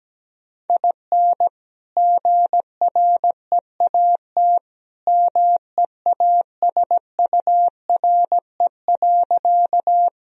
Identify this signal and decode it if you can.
Morse code, heard as a keyed tone, about 17 wpm, 700 Hz. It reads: IN GREAT MEASURE.
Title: Sentences from Top 1000 Words - Review Part 2 - 17wpm